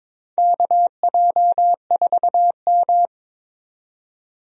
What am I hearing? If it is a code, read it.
Morse code, heard as a keyed tone, about 22 wpm, 700 Hz. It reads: KJ4M
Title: USA Call Signs - 4 Characters - 22wpm